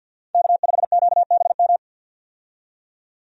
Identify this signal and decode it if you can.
Morse code, heard as a keyed tone, about 51 wpm, 700 Hz. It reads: K5QBM
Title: USA Call Signs - 5 Characters - 50wpm